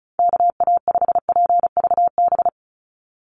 Morse code, heard as a keyed tone, about 34 wpm, 700 Hz. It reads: KA5PVB
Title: NA SOTA Chaser Callsigns 35wpm